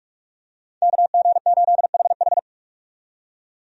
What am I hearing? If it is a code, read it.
Morse code, heard as a keyed tone, about 46 wpm, 700 Hz. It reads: KG8HH
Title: USA Call Signs - 5 Characters - 45wpm